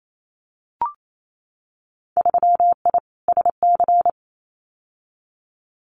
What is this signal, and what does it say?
Morse code, heard as a keyed tone, about 28 wpm, 690 Hz. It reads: 3I SC
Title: Winter Field Day Exchanges Encore 28wpm